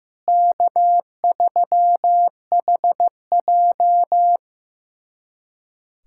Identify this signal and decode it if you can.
Morse code, heard as a keyed tone, about 15 wpm, 700 Hz. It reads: K3HJ
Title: USA Callsign DE Callsign - 4 Characters - 15wpm